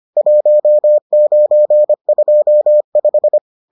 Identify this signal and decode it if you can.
Morse code, heard as a keyed tone, about 25 wpm, 600 Hz. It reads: 1925